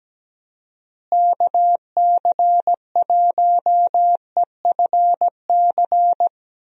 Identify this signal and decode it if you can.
Morse code, heard as a keyed tone, about 17 wpm, 700 Hz. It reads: KC1EFC
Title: USA Call Signs - 6 Characters - Repeated 3x - 17wpm